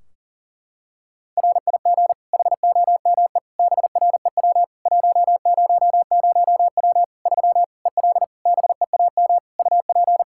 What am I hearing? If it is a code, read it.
Morse code, heard as a keyed tone, about 40 wpm, 700 Hz. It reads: RIG HOME BREW 100W 3 EL BEAM UP
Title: K1USN SST QSO Practice 40wpm